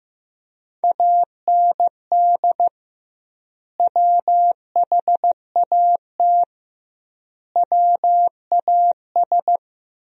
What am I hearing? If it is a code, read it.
Morse code, heard as a keyed tone, about 15 wpm, 700 Hz. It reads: AND WHAT WAS
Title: 2x Word-Spacing - Sentences from Top 200 Words - 15wpm